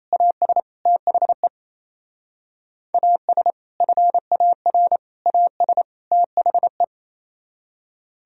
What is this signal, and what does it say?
Morse code, heard as a keyed tone, about 29 wpm, 700 Hz. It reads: AS THE AS FAR AS THE
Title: Speed-Racing - Sets of 4 Words 28wpm